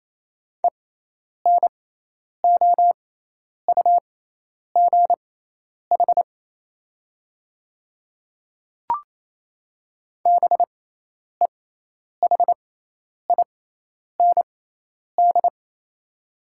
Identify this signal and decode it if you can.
Morse code, heard as a keyed tone, about 28 wpm, spaced as for 10 wpm, 700 Hz. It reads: ENOUGH BEHIND
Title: 6x Character Spacing - Top 500 Words - 28wpm